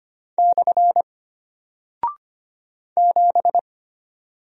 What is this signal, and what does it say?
Morse code, heard as a keyed tone, about 25 wpm, 700 Hz. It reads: / 7
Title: Single Character (Letter:Number:Punctuation:Prosign) Alt 25wpm